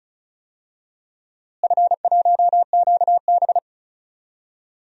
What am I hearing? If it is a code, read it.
Morse code, heard as a keyed tone, about 36 wpm, 700 Hz. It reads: F1QB
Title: DXCC-100 Call Signs - 4 Characters - Repeated 2x - 35wpm